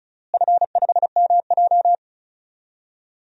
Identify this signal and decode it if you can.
Morse code, heard as a keyed tone, about 34 wpm, 700 Hz. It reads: F5MJ
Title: DXCC-100 Call Signs - 4 Characters - Repeated 2x - 35wpm